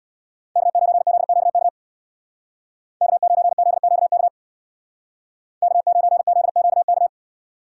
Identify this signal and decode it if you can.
Morse code, heard as a keyed tone, about 62 wpm, 700 Hz. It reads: K9ZQK K9ZQK K9ZQK
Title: Speed Racing - USA 1x3 Call Signs 50wpm